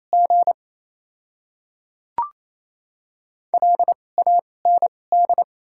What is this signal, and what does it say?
Morse code, heard as a keyed tone, about 28 wpm, spaced as for 8 wpm, 700 Hz. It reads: GLAND
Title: 2x Character Spacing - Top 200 Words - 28wpm